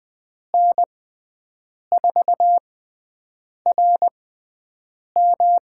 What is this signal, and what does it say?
Morse code, heard as a keyed tone, about 20 wpm, spaced as for 7 wpm, 700 Hz. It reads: N4RM
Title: USA Call Signs - 5 Characters - 6x Character Spacing - 20wpm